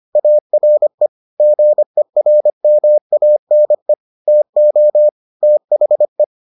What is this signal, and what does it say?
Morse code, heard as a keyed tone, about 25 wpm, 600 Hz. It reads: ARE GERMANE TO THE